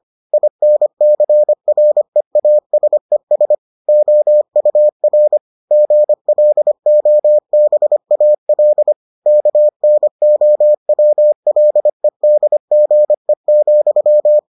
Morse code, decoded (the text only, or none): INCREASES OUR GLOBAL KNOWLEDGE,